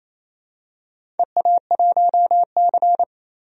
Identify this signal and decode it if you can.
Morse code, heard as a keyed tone, about 27 wpm, 700 Hz. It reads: EA1C
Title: DXCC-100 Call Signs - 4 Characters - 28wpm